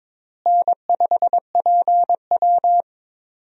Morse code, heard as a keyed tone, about 22 wpm, 700 Hz. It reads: N5PW
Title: USA Call Signs - 4 Characters - Repeated 2x - 22wpm